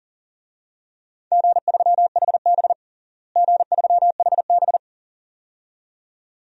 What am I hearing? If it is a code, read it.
Morse code, heard as a keyed tone, about 40 wpm, 700 Hz. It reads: G3HB G3HB
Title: DXCC-100 Call Signs - 4 Characters - Repeated 2x - 40wpm